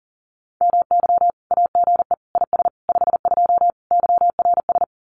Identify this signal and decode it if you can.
Morse code, heard as a keyed tone, about 43 wpm, 700 Hz. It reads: MY AGE IS 52 YRS
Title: Common QSO Phrases Encore 40wpm